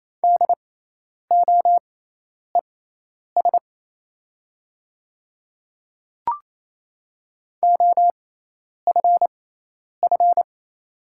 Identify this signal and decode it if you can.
Morse code, heard as a keyed tone, about 27 wpm, spaced as for 10 wpm, 690 Hz. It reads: DOES OFF
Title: 6x Character Spacing - Top 200 Words - 28wpm